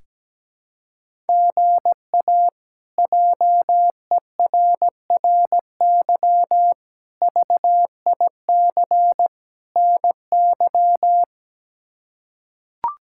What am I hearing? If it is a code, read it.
Morse code, heard as a keyed tone, about 17 wpm, 700 Hz. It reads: GA JERRY VIC NY
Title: K1USN SST QSO Practice - 17wpm